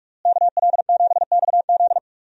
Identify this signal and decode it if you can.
Morse code, heard as a keyed tone, about 44 wpm, 700 Hz. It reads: KL7XZ
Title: DXCC-100 Call Signs - 5 Characters - Repeated 3x - 45wpm